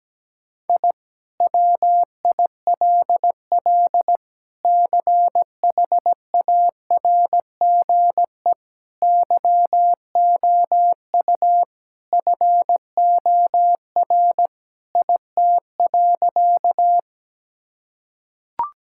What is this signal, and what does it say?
Morse code, heard as a keyed tone, about 17 wpm, 700 Hz. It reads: I WILL CHARGE YOU FOR IT.
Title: Sentences from Top 1000 Words - Review Part 6 - 17wpm